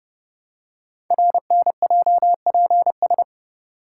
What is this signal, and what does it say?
Morse code, heard as a keyed tone, about 30 wpm, 700 Hz. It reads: RNJPS
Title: Code Groups - 5 Letters - 30wpm